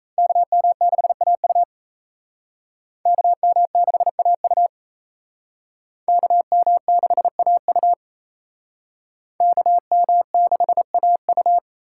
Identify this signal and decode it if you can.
Morse code, heard as a keyed tone, about 37 wpm, 700 Hz. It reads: KM6AU KM6AU KM6AU KM6AU